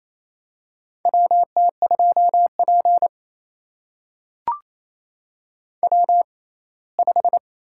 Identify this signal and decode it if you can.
Morse code, heard as a keyed tone, about 28 wpm, spaced as for 19 wpm, 700 Hz. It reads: WT2P W5
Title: USA Call Signs - 4 Characters - 6x Character Spacing - 28wpm